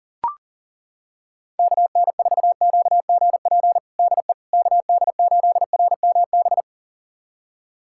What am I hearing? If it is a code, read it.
Morse code, heard as a keyed tone, about 40 wpm, 700 Hz. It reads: KN4QGP DE KD8RMB
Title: USA Callsign DE Callsign 40wpm